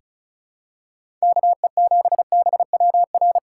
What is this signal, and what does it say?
Morse code, heard as a keyed tone, about 34 wpm, 700 Hz. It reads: KE7BWR
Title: USA Callsign DE Callsign - 6 Characters - 35wpm